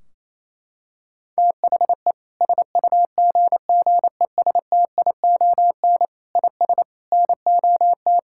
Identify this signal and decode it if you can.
Morse code, heard as a keyed tone, about 28 wpm, 700 Hz. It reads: THE SUGGESTION IS NOT
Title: Sentences from Top 1600 Words 28wpm